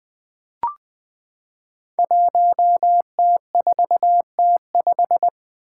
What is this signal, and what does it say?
Morse code, heard as a keyed tone, about 20 wpm, 700 Hz. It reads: 1T4T5
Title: Five Challenging Letter-Number Combinations 20wpm